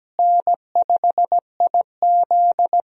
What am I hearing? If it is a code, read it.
Morse code, heard as a keyed tone, about 17 wpm, 700 Hz. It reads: N5IZ
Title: USA Call Signs - 4 Characters - Repeated 2x - 17wpm